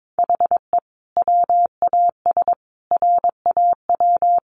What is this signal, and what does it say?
Morse code, heard as a keyed tone, about 22 wpm, 700 Hz. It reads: HE WAS RAW
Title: Sentences from Top 1500 Words 22wpm